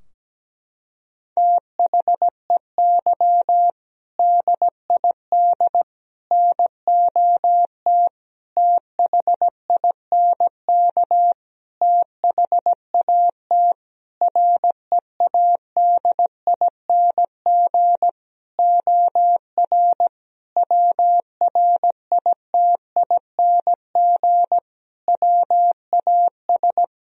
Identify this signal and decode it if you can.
Morse code, heard as a keyed tone, about 17 wpm, 700 Hz. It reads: THEY DID NOT THINK THAT READING OR WRITING WAS
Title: Sentences from Top 1000 Words - Review Part 3 - 17wpm